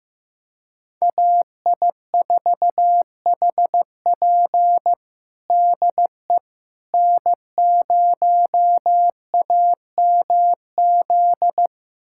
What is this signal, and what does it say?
Morse code, heard as a keyed tone, about 15 wpm, 700 Hz. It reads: AI4HP DE N0AMZ
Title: USA Callsign DE Callsign - 5 Characters - 15wpm